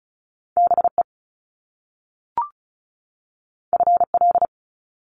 Morse code, heard as a keyed tone, about 36 wpm, 700 Hz. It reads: BE FL
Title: Top Two Letter Combinations in Words 35wpm